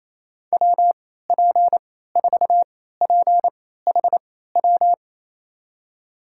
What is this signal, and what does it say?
Morse code, heard as a keyed tone, about 28 wpm, spaced as for 16 wpm, 700 Hz. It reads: WP4PHW
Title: USA Call Signs - 6 Characters - 3x Character Spacing - 28wpm